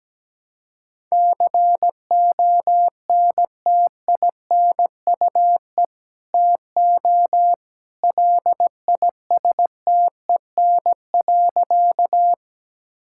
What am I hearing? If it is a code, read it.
Morse code, heard as a keyed tone, about 17 wpm, 700 Hz. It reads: CONTINUE TO LISTEN.
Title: Sentences from Top 1000 Words - Review Part 6 - 17wpm